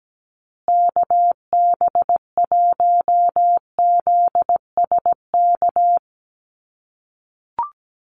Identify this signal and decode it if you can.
Morse code, heard as a keyed tone, about 17 wpm, 700 Hz. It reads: KB1ZSK
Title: USA Call Signs Repeated 2x 17wpm